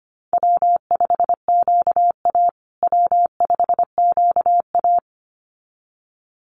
Morse code, heard as a keyed tone, about 25 wpm, 700 Hz. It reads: W5QA W5QA
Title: USA Call Signs Repeated 2x 25wpm